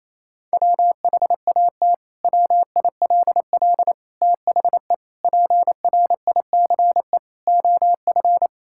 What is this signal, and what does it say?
Morse code, heard as a keyed tone, about 28 wpm, 700 Hz. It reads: WHAT WILL THE PRICE OF
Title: Sentences from Top 1000 Words - Review Part 5 - 28wpm